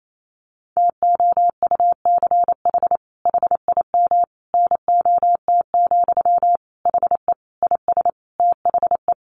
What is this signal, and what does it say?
Morse code, heard as a keyed tone, about 28 wpm, 700 Hz. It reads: TOUCH HIM NOT, HE IS THE
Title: Sentences from Top 1200 Words 28wpm